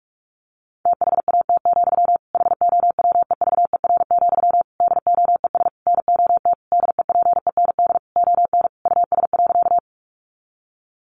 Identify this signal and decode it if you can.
Morse code, heard as a keyed tone, about 44 wpm, 700 Hz. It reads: THAT, HOWEVER, DOES NOT DEPEND ON US.
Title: Sentences from Top 1300 Words 45wpm